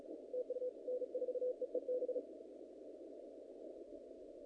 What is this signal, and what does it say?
Morse code, heard as a keyed tone, about 35 wpm, 500 Hz. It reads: TU N4EEB
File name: contests / naqp-cw-k5tr-op-k5pi- -2000